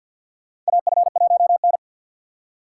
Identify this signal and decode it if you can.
Morse code, heard as a keyed tone, about 48 wpm, 700 Hz. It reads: AF1N
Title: USA Call Signs - 4 Characters - 3x Character Spacing - 50wpm